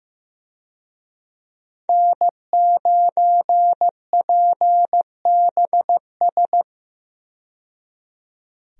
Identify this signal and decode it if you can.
Morse code, heard as a keyed tone, about 15 wpm, 700 Hz. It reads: N9PBS
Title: USA Call Signs - 5 Characters - 15wpm